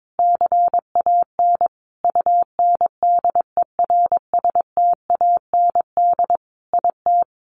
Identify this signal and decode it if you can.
Morse code, heard as a keyed tone, about 22 wpm, 700 Hz. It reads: CAN UNDERSTAND IT
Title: Sentences from Top 1300 Words 22wpm